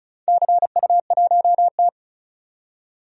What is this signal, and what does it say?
Morse code, heard as a keyed tone, about 34 wpm, 700 Hz. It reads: CU1T
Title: DXCC-100 Call Signs - 4 Characters - 35wpm